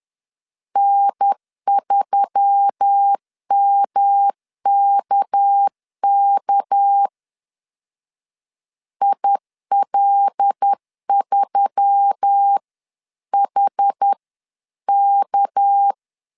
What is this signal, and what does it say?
Morse code, heard as a keyed tone, about 10 wpm, 800 Hz. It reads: N3MKK IL3HK
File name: LEZIONI / 6 PKX0 / 6 PK 10wpm 2436